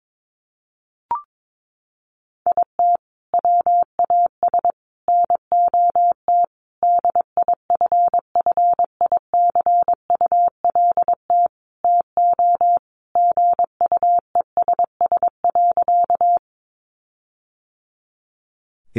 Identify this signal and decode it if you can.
Morse code, heard as a keyed tone, about 22 wpm, 700 Hz. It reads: IT WAS NOT DIFFICULT TO GUESS.
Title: Sentences from Top 1500 Words 22wpm